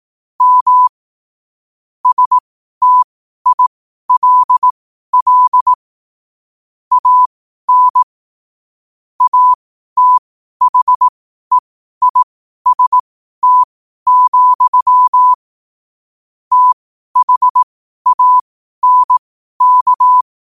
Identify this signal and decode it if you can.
Morse code, heard as a keyed tone, about 18 wpm, spaced as for 12 wpm, 1000 Hz. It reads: M STILL AN ATHEIST, THANK
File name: qotdfiles / 12wpm